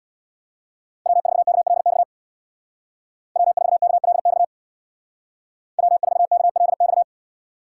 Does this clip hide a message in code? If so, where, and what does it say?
Yes: Morse code from 1.1 to 7.0 s, heard as a keyed tone, about 64 wpm, 700 Hz. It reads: W4KLX W4KLX W4KLX